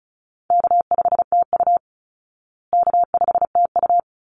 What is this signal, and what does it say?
Morse code, heard as a keyed tone, about 34 wpm, 700 Hz. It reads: K5TU K5TU
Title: Easy Call Signs Repeated 2x - 35wpm